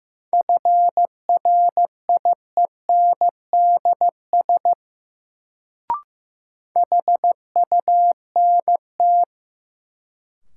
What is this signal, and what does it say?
Morse code, heard as a keyed tone, about 15 wpm, 700 Hz. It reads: FRIENDS HUNT
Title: Sentences from Top 1600 Words - New Word Drill Alt - 15wpm